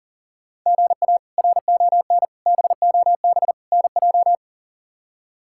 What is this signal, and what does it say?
Morse code, heard as a keyed tone, about 38 wpm, 700 Hz. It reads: GA RON BOB NJ
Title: K1USN SST QSO Practice - 40wpm